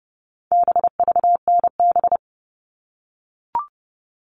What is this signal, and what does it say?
Morse code, heard as a keyed tone, about 30 wpm, 700 Hz. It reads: BVNB
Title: All Cut Numbers - 4-Digit 30wpm